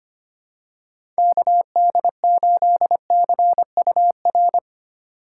Morse code, heard as a keyed tone, about 25 wpm, 700 Hz. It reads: KD8CUR
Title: USA Call Signs - 6 Characters - 4x Character Spacing - 25wpm